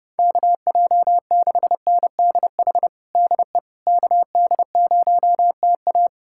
Morse code, heard as a keyed tone, about 30 wpm, 700 Hz. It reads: KJ6NDH DE KD0TA
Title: USA Callsign DE Callsign 30wpm